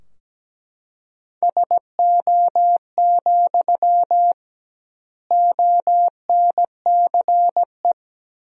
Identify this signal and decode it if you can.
Morse code, heard as a keyed tone, about 17 wpm, 700 Hz. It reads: SO, ONCE
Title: 2x Word-Spacing - Sentences from Top 300 Words - 17wpm